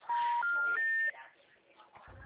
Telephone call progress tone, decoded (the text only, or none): special information tone